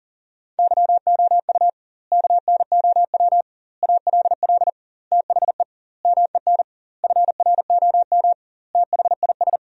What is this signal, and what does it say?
Morse code, heard as a keyed tone, about 40 wpm, 700 Hz. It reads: YOU KNOW ALL THE MEN FROM THIS
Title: 2x Word-Spacing - Sentences from Top 300 Words - 40wpm